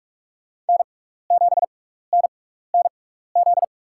Morse code, heard as a keyed tone, about 44 wpm, spaced as for 15 wpm, 700 Hz. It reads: N7NNZ
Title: USA Call Signs - 5 Characters - 6x Character Spacing - 45wpm